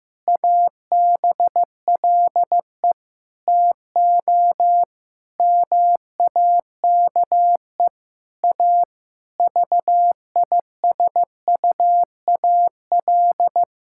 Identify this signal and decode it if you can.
Morse code, heard as a keyed tone, about 15 wpm, 700 Hz. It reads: ABLE TO MAKE A VISUAL
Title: Sentences from Top 1000 Words - Review Part 6 - 15wpm